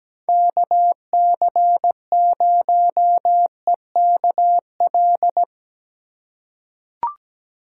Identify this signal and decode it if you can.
Morse code, heard as a keyed tone, about 17 wpm, 700 Hz. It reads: KC0EKL
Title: USA Call Signs - 6 Characters - 2x Character Spacing - 17wpm